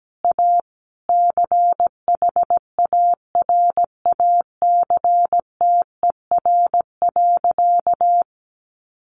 Morse code, heard as a keyed tone, about 17 wpm, 700 Hz. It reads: A CHARACTER.
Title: Sentences from Top 1200 Words 17wpm